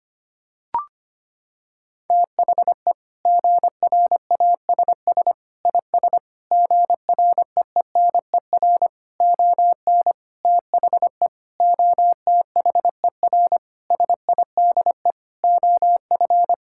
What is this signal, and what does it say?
Morse code, heard as a keyed tone, about 25 wpm, 700 Hz. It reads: THE GRASS IS GREENER ON THE OTHER SIDE OF
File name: practice / 25wpm / Sentences with 4-Word Phrases - 25wpm